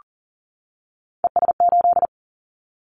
Morse code, heard as a keyed tone, about 40 wpm, 700 Hz. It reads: ES8
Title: Callsign Prefixes for Top 50 DXCC - 40wpm